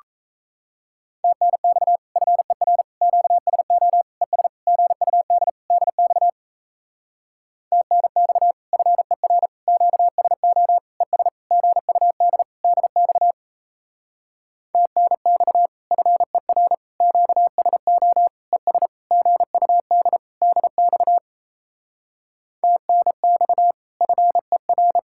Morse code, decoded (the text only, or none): TNX FER QSO ES GUD DX TNX FER QSO ES GUD DX TNX FER QSO ES GUD DX TNX FER